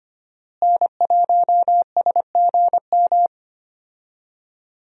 Morse code, heard as a keyed tone, about 25 wpm, 700 Hz. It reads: N1SGM
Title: USA Call Signs - 5 Characters - Repeated 2x - 25wpm